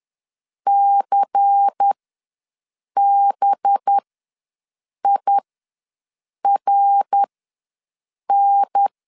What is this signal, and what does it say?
Morse code, heard as a keyed tone, about 10 wpm, spaced as for 6 wpm, 800 Hz. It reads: CBIRN
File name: LEZIONI / 6 PKX0 / 6 PKX0 ws 10wpm 4220